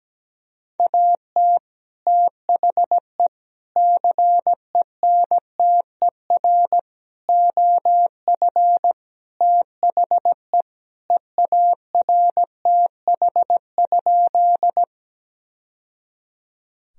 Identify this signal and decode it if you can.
Morse code, heard as a keyed tone, about 17 wpm, 700 Hz. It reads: AT THE CENTER OF THE EARTH?
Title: Sentences from Top 1000 Words - Review Part 2 - 17wpm